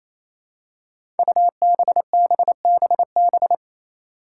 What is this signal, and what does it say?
Morse code, heard as a keyed tone, about 28 wpm, 700 Hz. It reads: UBBBB
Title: Five Challenging Letter-Number Combinations 28wpm